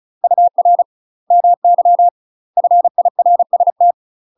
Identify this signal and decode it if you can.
Morse code, heard as a keyed tone, about 35 wpm, 700 Hz. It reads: UR MY FIRST
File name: UK MASTER QSOs TRAINING / 35WPM / UK MASTER QSOs 0041